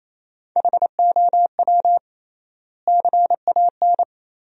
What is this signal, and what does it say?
Morse code, heard as a keyed tone, about 27 wpm, 700 Hz. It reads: HOW CAN
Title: 3x Word-Spacing - Sentences from Top 100 Words - 28wpm